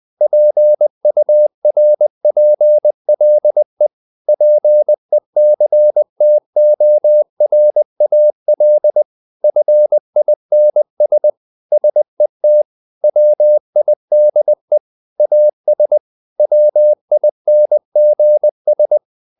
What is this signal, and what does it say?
Morse code, heard as a keyed tone, about 20 wpm, 600 Hz. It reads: PURPLE PECTORAL FINS SET WIDE AS WINGS
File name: the old man and the sea / the old man and the sea part 1 20wpm